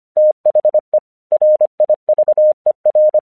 Morse code, heard as a keyed tone, about 25 wpm, 610 Hz. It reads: THE RIVER
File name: 1 English 25wpm / AQC06